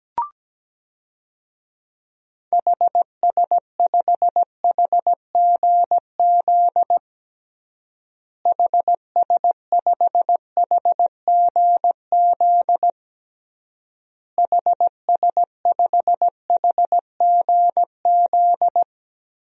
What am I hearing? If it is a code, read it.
Morse code, heard as a keyed tone, about 17 wpm, 700 Hz. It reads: HS5HGZ HS5HGZ HS5HGZ
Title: DXCC-100 Call Signs - 6 Characters - Repeated 3x - 17wpm